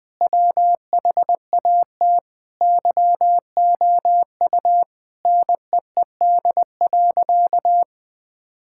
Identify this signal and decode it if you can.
Morse code, heard as a keyed tone, about 20 wpm, 700 Hz. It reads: WHAT YOU NEED.
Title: Sentences with 5-Word Phrases - Spoken Beginning - 20wpm